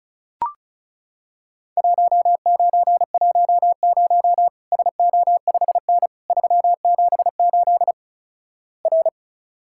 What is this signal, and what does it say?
Morse code, heard as a keyed tone, about 36 wpm, 700 Hz. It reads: 1910 SO5N 378 R
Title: Worked All Europe DX Contest - QTC Exchange - 35wpm